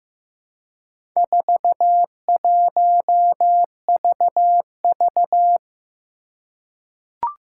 Morse code, heard as a keyed tone, about 15 wpm, 700 Hz. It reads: 41VV
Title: Four Challenging Letter-Number Combinations 15wpm